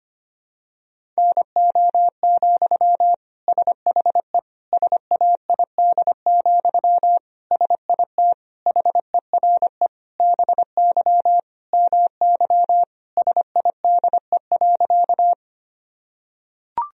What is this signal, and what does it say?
Morse code, heard as a keyed tone, about 25 wpm, 700 Hz. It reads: NO, SHE SAID, SIT HERE BY MY SIDE.
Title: Sentences from Top 1000 Words - Review Part 1 - 25wpm